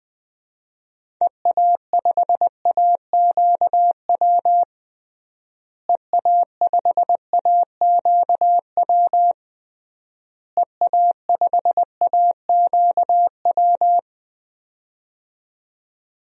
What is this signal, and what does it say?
Morse code, heard as a keyed tone, about 20 wpm, 700 Hz. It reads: EA5AQW EA5AQW EA5AQW
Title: DXCC-100 Call Signs - 6 Characters - Repeated 3x - 20wpm